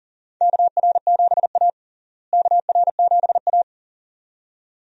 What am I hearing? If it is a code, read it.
Morse code, heard as a keyed tone, about 40 wpm, 700 Hz. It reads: KR7A KR7A
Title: USA Call Signs - 4 Characters - Repeated 2x - 40wpm